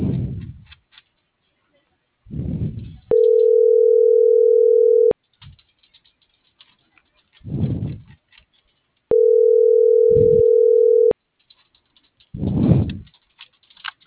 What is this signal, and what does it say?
Ringback tone (North America), 440 + 480 Hz.